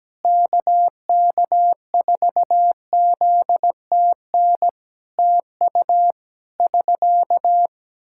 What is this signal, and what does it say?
Morse code, heard as a keyed tone, about 17 wpm, 700 Hz. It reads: KK4ZTN TU <SK>
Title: Practice QSO - Exchange 4 - Ending - 17wpm